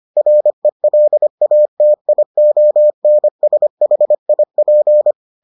Morse code, heard as a keyed tone, about 25 wpm, 600 Hz. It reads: RELATIONSHIP